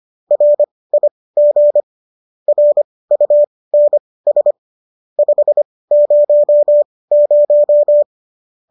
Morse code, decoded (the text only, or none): RIG RUNS 500